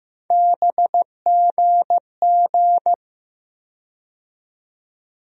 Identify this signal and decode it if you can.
Morse code, heard as a keyed tone, about 15 wpm, 700 Hz. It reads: BGG
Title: USA Call Sign Postfixes - 3 Letters 15wpm